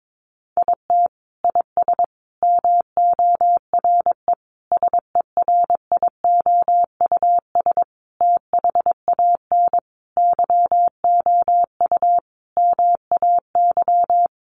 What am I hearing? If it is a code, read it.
Morse code, heard as a keyed tone, about 22 wpm, 700 Hz. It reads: IT IS MORE SERIOUS THAN YOU MAY
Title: Sentences from Top 1100 Words 22wpm